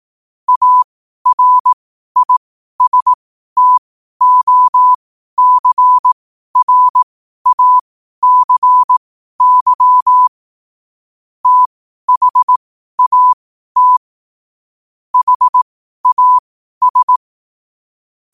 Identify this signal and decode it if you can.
Morse code, heard as a keyed tone, about 18 wpm, spaced as for 12 wpm, 1000 Hz. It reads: ARISTOCRACY THAT HAS
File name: qotdfiles / 12wpm